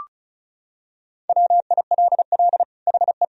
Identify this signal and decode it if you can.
Morse code, heard as a keyed tone, about 34 wpm, 700 Hz. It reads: WILL HE
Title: Sentences from Top 1000 Words - Review Part 2 - 35wpm